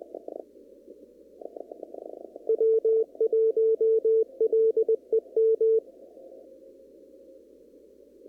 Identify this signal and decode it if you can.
Morse code, heard as a keyed tone, about 20 wpm, 450 Hz. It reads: W1LEM
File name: CW / kt5j wpx cw-op-k5pi